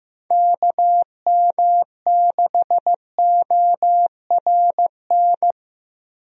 Morse code, heard as a keyed tone, about 15 wpm, 700 Hz. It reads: KM6ORN
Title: USA Call Signs - 6 Characters - Repeated 2x - 15wpm